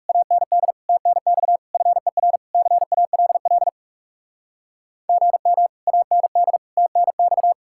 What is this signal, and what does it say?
Morse code, heard as a keyed tone, about 37 wpm, 700 Hz. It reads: AND TNX FER CALL GM AND TNX
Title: Speed-Racing Common QSO Phrases Encore - 30wpm